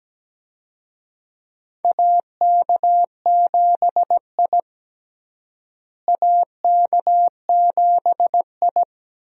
Morse code, heard as a keyed tone, about 17 wpm, 700 Hz. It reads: AK7I AK7I
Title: USA Call Signs - 4 Characters - Repeated 2x - 17wpm